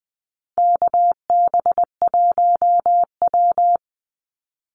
Morse code, heard as a keyed tone, about 20 wpm, 700 Hz. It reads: KB1W
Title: Call Signs Repeated 3x - 20wpm